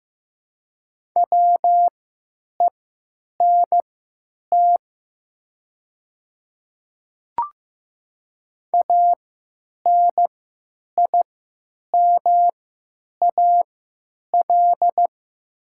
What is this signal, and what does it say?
Morse code, heard as a keyed tone, about 15 wpm, spaced as for 9 wpm, 700 Hz. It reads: WENT ANIMAL
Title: 3x Character Spacing - Top 200 Words - 15wpm